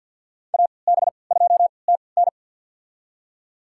Morse code, heard as a keyed tone, about 46 wpm, spaced as for 27 wpm, 700 Hz. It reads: AB2TN
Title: USA Call Signs - 5 Characters - 3x Character Spacing - 50wpm